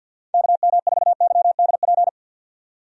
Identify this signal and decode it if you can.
Morse code, heard as a keyed tone, about 51 wpm, 700 Hz. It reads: KM4YDP
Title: USA Call Signs - 6 Characters - 50wpm